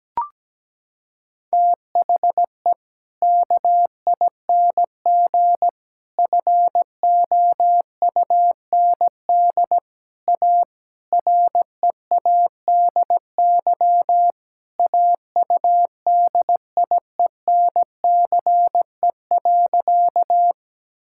Morse code, heard as a keyed tone, about 17 wpm, 700 Hz. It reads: THE KING FOUND A READY AUDIENCE.
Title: Sentences from Top 1000 Words - Review Part 5 - 17wpm